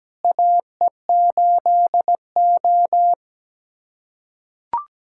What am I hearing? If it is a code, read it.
Morse code, heard as a keyed tone, about 17 wpm, 700 Hz. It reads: AE8O
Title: USA Callsign DE Callsign - 4 Characters - 17wpm